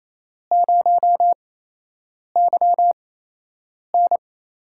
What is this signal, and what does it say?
Morse code, heard as a keyed tone, about 28 wpm, spaced as for 8 wpm, 700 Hz. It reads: 0YN
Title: USA Call Signs - 8x Character Spacing - 28wpm